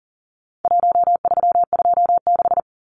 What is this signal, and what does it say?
Morse code, heard as a keyed tone, about 40 wpm, 700 Hz. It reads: 1326
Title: Numbers - 4-Digit 40wpm